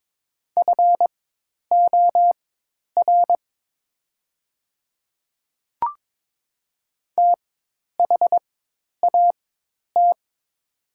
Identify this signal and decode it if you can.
Morse code, heard as a keyed tone, about 22 wpm, spaced as for 10 wpm, 700 Hz. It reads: FOR THAT
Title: 4x Character Spacing - Top 100 Words - 22wpm